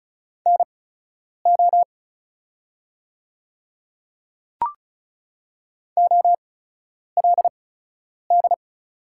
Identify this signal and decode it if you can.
Morse code, heard as a keyed tone, about 35 wpm, spaced as for 10 wpm, 690 Hz. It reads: NO OLD
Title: 8x Character Spacing - Top 300 Words - 35wpm